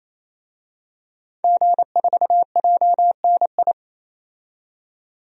Morse code, heard as a keyed tone, about 28 wpm, 700 Hz. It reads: G4JNI
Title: DXCC-100 Call Signs - 5 Characters - 28wpm